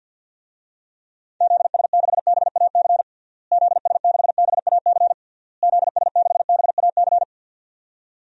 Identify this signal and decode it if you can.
Morse code, heard as a keyed tone, about 51 wpm, 700 Hz. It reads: ZS6BAC ZS6BAC ZS6BAC
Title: DXCC-100 Call Signs - 6 Characters - Repeated 3x - 50wpm